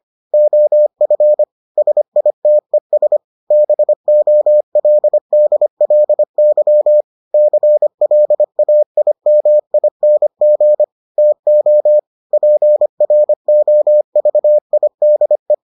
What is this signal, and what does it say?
Morse code, heard as a keyed tone, about 25 wpm, 600 Hz. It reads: OF SITES BOLDLY CLAIMING TO PROVIDE